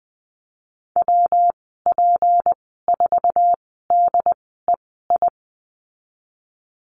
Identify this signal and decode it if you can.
Morse code, heard as a keyed tone, about 20 wpm, spaced as for 15 wpm, 700 Hz. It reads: WP4DEI
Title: 167 Lesson 27g - Number 7 - Call Signs 20wpm